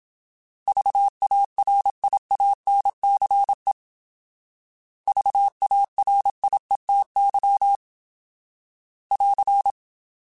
Morse code, heard as a keyed tone, about 27 wpm, 800 Hz. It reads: VARIANCE VARIETY <AR>
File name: Alphabetical / Alphabetical mp3s / Vs @25